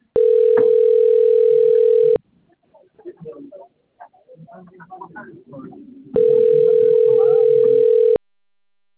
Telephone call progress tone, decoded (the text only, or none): ringback tone